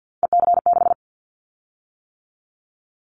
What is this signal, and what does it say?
Morse code, heard as a keyed tone, about 50 wpm, 700 Hz. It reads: EC6
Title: Callsign Prefixes for Top 100 DXCC - 50wpm